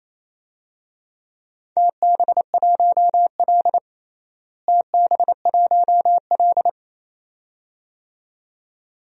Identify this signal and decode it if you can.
Morse code, heard as a keyed tone, about 28 wpm, 700 Hz. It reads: TB1L TB1L
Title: DXCC-100 Call Signs - 4 Characters - Repeated 2x - 28wpm